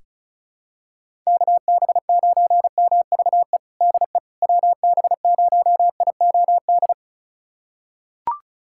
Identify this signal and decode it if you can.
Morse code, heard as a keyed tone, about 35 wpm, 700 Hz. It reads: KB9MVE DE WB0IOD
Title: USA Callsign DE Callsign 35wpm